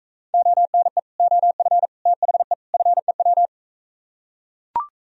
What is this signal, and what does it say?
Morse code, heard as a keyed tone, about 45 wpm, 700 Hz. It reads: ONE OF THE FEW